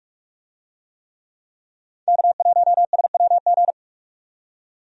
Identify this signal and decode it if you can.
Morse code, heard as a keyed tone, about 46 wpm, 700 Hz. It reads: K1SWG
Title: USA Call Signs - 5 Characters - 45wpm